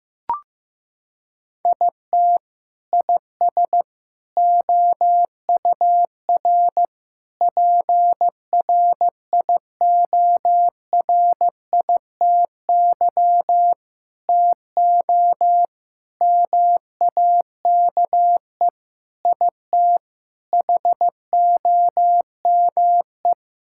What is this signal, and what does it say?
Morse code, heard as a keyed tone, about 15 wpm, 700 Hz. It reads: IT IS OUR PRIORITY TO MAKE IT HOME
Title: Sentences from Top 1600 Words 15wpm